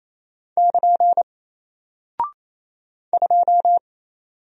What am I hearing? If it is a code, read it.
Morse code, heard as a keyed tone, about 27 wpm, 700 Hz. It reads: <KN> 2
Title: Single Character (Letter:Number:Punctuation:Prosign) 28wpm